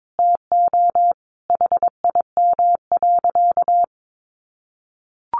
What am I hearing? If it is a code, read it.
Morse code, heard as a keyed tone, about 22 wpm, 700 Hz. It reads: TO HIM.
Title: Sentences from Top 800 Words 22wpm